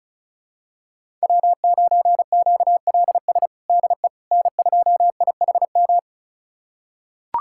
Morse code, heard as a keyed tone, about 36 wpm, 700 Hz. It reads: W9QLS DE N2IHM
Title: USA Callsign DE Callsign - 5 Characters - 35wpm